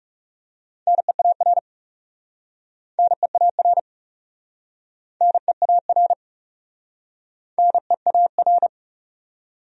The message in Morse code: NEAR NEAR NEAR NEAR

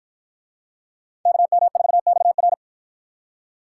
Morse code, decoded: KM4XR

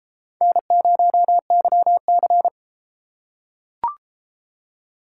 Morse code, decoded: N0YC

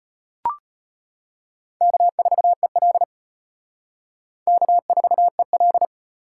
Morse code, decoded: K4EL K4EL

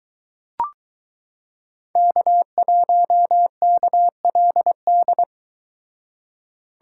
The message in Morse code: K1KLD

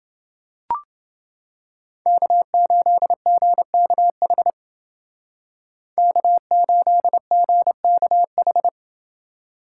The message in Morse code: K8GKH K8GKH

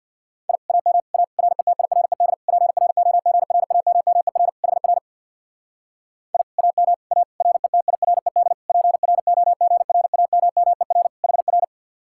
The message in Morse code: I AM A RETIRED PROGRAMMER HR I AM A RETIRED PROGRAMMER HR

